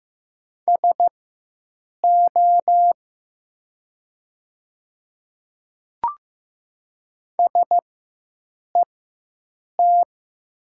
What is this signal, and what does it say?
Morse code, heard as a keyed tone, about 15 wpm, spaced as for 7 wpm, 700 Hz. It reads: SO SET
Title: 4x Character Spacing - Top 200 Words - 15wpm